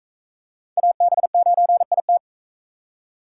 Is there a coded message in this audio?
Yes: Morse code from 0.8 to 2.2 s, heard as a keyed tone, about 42 wpm, 700 Hz. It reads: AB9IT